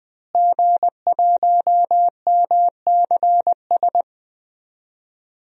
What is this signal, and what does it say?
Morse code, heard as a keyed tone, about 20 wpm, 700 Hz. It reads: G1MCS
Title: DXCC-100 Call Signs - 5 Characters - Repeated 3x - 20wpm